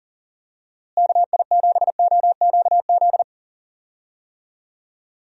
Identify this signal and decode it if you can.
Morse code, heard as a keyed tone, about 40 wpm, 700 Hz. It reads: KI7OQZ
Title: USA Call Signs - 6 Characters - Repeated 3x - 40wpm